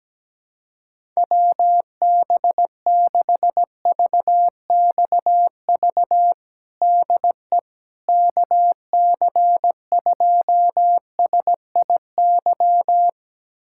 Morse code, heard as a keyed tone, about 17 wpm, 700 Hz. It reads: WB6VXV DE KC2SIY
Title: USA Callsign DE Callsign 17wpm